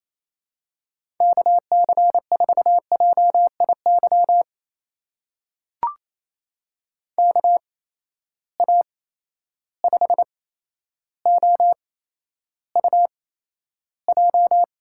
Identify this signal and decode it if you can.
Morse code, heard as a keyed tone, about 28 wpm, spaced as for 12 wpm, 700 Hz. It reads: KC4JIY KA5OUJ
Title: USA Call Signs - 6 Characters - 8x Character Spacing - 28wpm